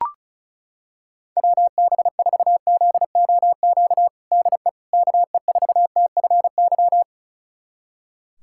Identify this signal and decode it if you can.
Morse code, heard as a keyed tone, about 34 wpm, 700 Hz. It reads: WB4ZOQ DE KE4TFY
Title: USA Callsign DE Callsign 35wpm